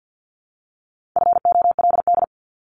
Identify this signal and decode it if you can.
Morse code, heard as a keyed tone, about 50 wpm, 700 Hz. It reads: FOLD
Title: Top 1751-2000 Words 50wpm